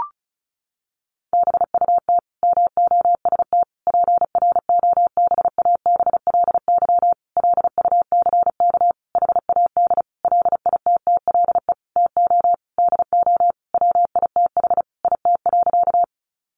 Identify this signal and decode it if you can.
Morse code, heard as a keyed tone, about 34 wpm, 700 Hz. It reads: BUT MOST PROBABLY LUCK HAD LITTLE TO DO WITH IT.
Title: Sentences from Top 1500 Words 35wpm